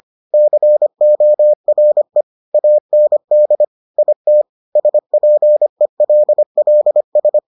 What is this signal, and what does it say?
Morse code, heard as a keyed tone, about 25 wpm, 600 Hz. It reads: CORE AND IT SPELLS